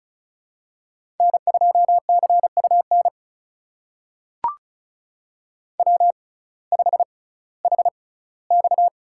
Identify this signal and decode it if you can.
Morse code, heard as a keyed tone, about 35 wpm, spaced as for 19 wpm, 700 Hz. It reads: N2CUN W5HX
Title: USA Call Signs - 6x Character Spacing - 35wpm